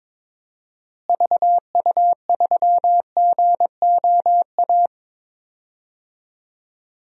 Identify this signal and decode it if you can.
Morse code, heard as a keyed tone, about 22 wpm, 700 Hz. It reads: VU3GOA
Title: DXCC-100 Call Signs - 6 Characters - Repeated 2x - 22wpm